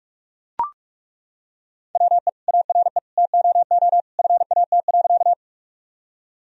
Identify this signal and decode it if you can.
Morse code, heard as a keyed tone, about 42 wpm, 700 Hz. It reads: WE ARE TOO FAT.
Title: Sentences from Top 1000 Words - Review Part 4 - 45wpm